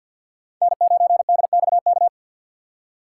Morse code, heard as a keyed tone, about 47 wpm, 690 Hz. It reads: N9DXK
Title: USA Call Signs - 5 Characters - 4x Character Spacing - 50wpm